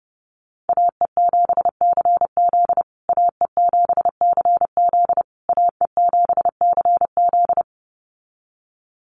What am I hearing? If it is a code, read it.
Morse code, heard as a keyed tone, about 30 wpm, 700 Hz. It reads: AE7CZ AE7CZ AE7CZ
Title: USA Call Signs Repeated 3x 30wpm